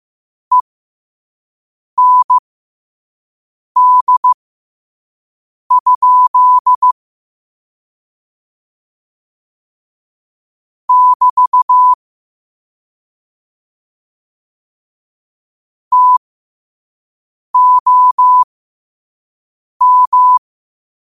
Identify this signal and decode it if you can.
Morse code, heard as a keyed tone, about 15 wpm, spaced as for 5 wpm, 1000 Hz. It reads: END? <BT> TOM
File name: qotdfiles / 5wpm